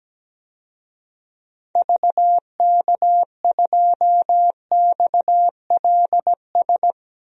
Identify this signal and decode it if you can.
Morse code, heard as a keyed tone, about 17 wpm, 700 Hz. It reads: VK2XLS
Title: DXCC-100 Call Signs - 6 Characters - Repeated 3x - 17wpm